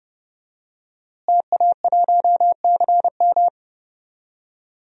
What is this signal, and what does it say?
Morse code, heard as a keyed tone, about 31 wpm, 700 Hz. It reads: TA1CM
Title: DXCC-100 Call Signs - 5 Characters - Repeated 2x - 30wpm